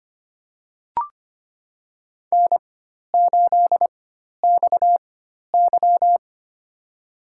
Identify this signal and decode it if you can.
Morse code, heard as a keyed tone, about 25 wpm, spaced as for 12 wpm, 700 Hz. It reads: N8XY
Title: USA Call Signs - 4x Character Spacing - 25wpm